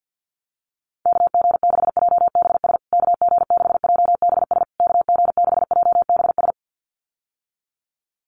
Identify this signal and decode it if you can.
Morse code, heard as a keyed tone, about 50 wpm, 700 Hz. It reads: KG6JBS KG6JBS KG6JBS
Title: USA Call Signs Repeated 3x 50wpm